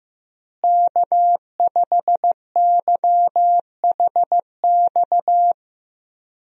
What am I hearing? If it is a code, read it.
Morse code, heard as a keyed tone, about 15 wpm, 700 Hz. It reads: K5YHX
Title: USA Call Signs - 5 Characters - Repeated 3x - 15wpm